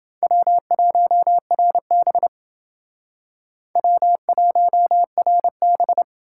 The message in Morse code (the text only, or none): W1RB W1RB